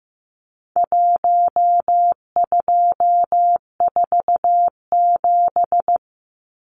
Morse code, heard as a keyed tone, about 15 wpm, 700 Hz. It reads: 1247
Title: Numbers - Contest Style - 4-Digit 15wpm